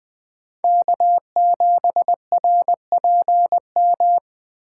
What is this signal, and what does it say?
Morse code, heard as a keyed tone, about 20 wpm, 700 Hz. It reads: K7RPM
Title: USA Call Signs - 5 Characters - 2x Character Spacing - 20wpm